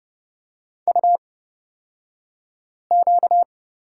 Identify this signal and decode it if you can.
Morse code, heard as a keyed tone, about 30 wpm, spaced as for 5 wpm, 700 Hz. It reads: UQ
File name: Lesson 38d - Letter q - Two Characters 30wpm